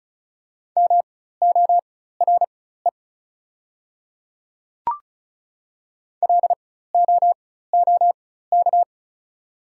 Morse code, heard as a keyed tone, about 34 wpm, spaced as for 16 wpm, 690 Hz. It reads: MORE LOOK
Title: 4x Character Spacing - Top 100 Words - 35wpm